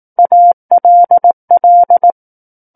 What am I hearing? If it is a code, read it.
Morse code, heard as a keyed tone, about 18 wpm, 700 Hz. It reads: ALL
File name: 18 WPM